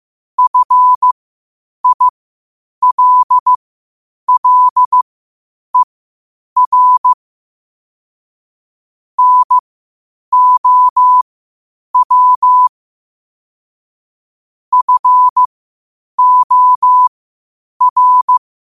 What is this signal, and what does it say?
Morse code, heard as a keyed tone, about 15 wpm, spaced as for 8 wpm, 1000 Hz. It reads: FILLER NOW FOR
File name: qotdfiles / 8wpm